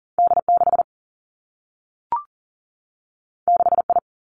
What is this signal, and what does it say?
Morse code, heard as a keyed tone, about 40 wpm, 700 Hz. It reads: D 6 6 I
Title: Two Challenging Letter-Number Combinations 40wpm